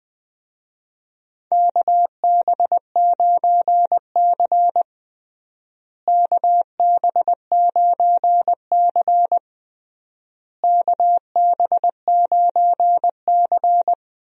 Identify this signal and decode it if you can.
Morse code, heard as a keyed tone, about 21 wpm, 700 Hz. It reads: KB9C KB9C KB9C
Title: USA Call Signs - 4 Characters - Repeated 3x - 20wpm